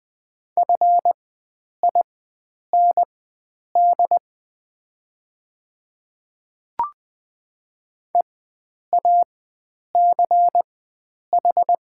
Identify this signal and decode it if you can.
Morse code, heard as a keyed tone, about 20 wpm, spaced as for 9 wpm, 700 Hz. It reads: FIND EACH
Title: 4x Character Spacing - Top 100 Words - 20wpm